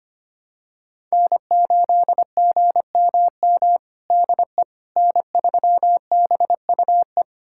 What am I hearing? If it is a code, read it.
Morse code, heard as a keyed tone, about 25 wpm, 700 Hz. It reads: N8GMM DE N3BUE
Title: USA Callsign DE Callsign - 5 Characters - 25wpm